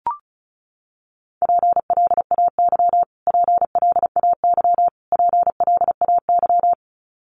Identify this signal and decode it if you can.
Morse code, heard as a keyed tone, about 34 wpm, 700 Hz. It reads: PLAY PLAY PLAY
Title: Top 300 Words Repeated 35wpm